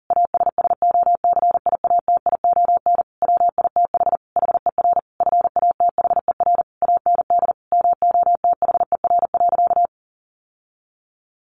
Morse code, heard as a keyed tone, about 40 wpm, 700 Hz. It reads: ASSOCIATION WITH HER FATHER AND MOTHER.
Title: Sentences from Top 1100 Words 40wpm